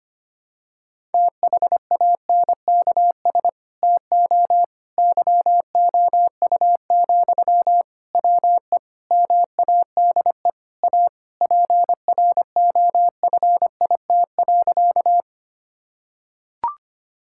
Morse code, heard as a keyed tone, about 25 wpm, 700 Hz. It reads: THANKS TO YOU, WE MADE A PROFIT.
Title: Sentences from Top 1000 Words - Review Part 5 - 25wpm